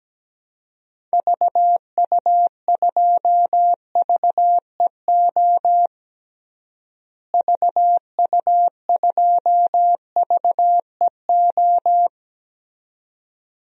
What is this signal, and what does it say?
Morse code, heard as a keyed tone, about 17 wpm, 700 Hz. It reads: VU2VEO VU2VEO
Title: DXCC-100 Call Signs - 6 Characters - Repeated 2x - 17wpm